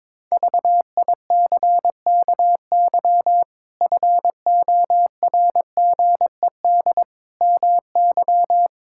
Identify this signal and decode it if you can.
Morse code, heard as a keyed tone, about 21 wpm, 700 Hz. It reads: VICKY FORGED MY
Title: Pangram Challenge v2 - 22wpm